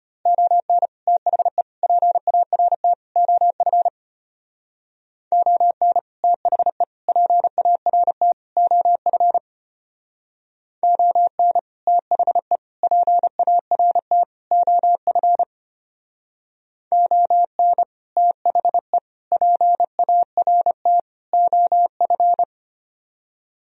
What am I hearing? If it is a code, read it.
Morse code, heard as a keyed tone, about 30 wpm, 700 Hz. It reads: ON THE PART OF ON THE PART OF ON THE PART OF ON THE PART OF
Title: Speed-Racing - Sets of 4 Words 25wpm